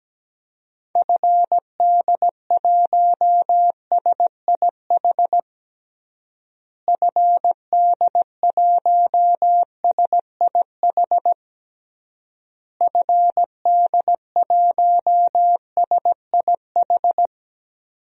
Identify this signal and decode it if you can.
Morse code, heard as a keyed tone, about 17 wpm, 700 Hz. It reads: FD1SIH FD1SIH FD1SIH
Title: DXCC-100 Call Signs - 6 Characters - Repeated 3x - 17wpm